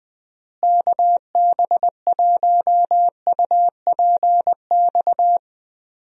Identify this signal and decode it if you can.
Morse code, heard as a keyed tone, about 20 wpm, 700 Hz. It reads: KB1UPX
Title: USA Call Signs - 6 Characters - Repeated 3x - 20wpm